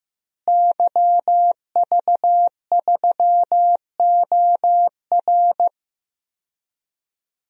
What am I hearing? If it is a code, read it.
Morse code, heard as a keyed tone, about 16 wpm, 700 Hz. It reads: YV3OR
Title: DXCC-100 Call Signs - 5 Characters - Repeated 3x - 15wpm